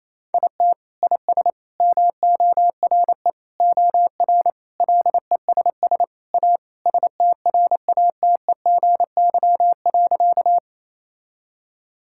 Morse code, heard as a keyed tone, about 28 wpm, 700 Hz. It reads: IT IS MORE OR LESS A STRATEGY.
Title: Sentences from Top 1000 Words - Review Part 5 - 28wpm